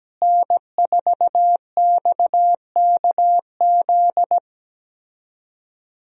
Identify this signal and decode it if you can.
Morse code, heard as a keyed tone, about 17 wpm, 700 Hz. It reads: N4XKZ
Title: USA Call Signs - 5 Characters - Repeated 2x - 17wpm